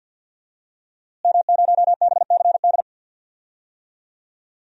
Morse code, heard as a keyed tone, about 48 wpm, 700 Hz. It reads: M0BCD
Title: DXCC-100 Call Signs - 5 Characters - Repeated 3x - 50wpm